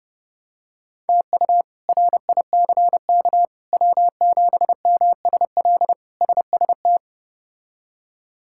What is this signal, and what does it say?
Morse code, heard as a keyed tone, about 30 wpm, 700 Hz. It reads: TU RICK W7MSL SST
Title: K1USN SST QSO Practice - 30wpm